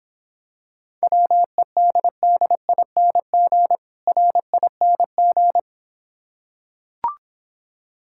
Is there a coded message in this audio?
Yes: Morse code from 1.0 to 5.6 s, heard as a keyed tone, about 26 wpm, 700 Hz. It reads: WEDDING RING